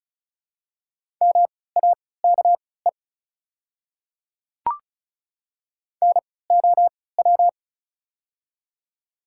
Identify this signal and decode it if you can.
Morse code, heard as a keyed tone, about 34 wpm, spaced as for 20 wpm, 700 Hz. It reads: MAKE NOW
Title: 3x Character Spacing - Top 300 Words - 35wpm